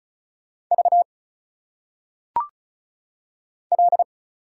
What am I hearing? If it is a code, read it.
Morse code, heard as a keyed tone, about 37 wpm, 700 Hz. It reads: V L
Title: Single Letters - Phonetic Pronunciation 35wpm